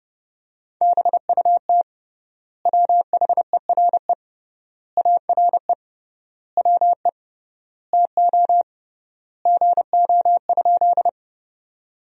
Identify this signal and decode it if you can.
Morse code, heard as a keyed tone, about 30 wpm, 700 Hz. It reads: BUT WHERE ARE WE TO GO?
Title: 3x Word-Spacing - Sentences from Top 200 Words - 30wpm